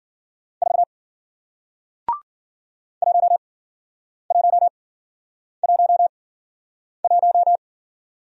Morse code, heard as a keyed tone, about 54 wpm, spaced as for 9 wpm, 700 Hz. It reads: 4 1111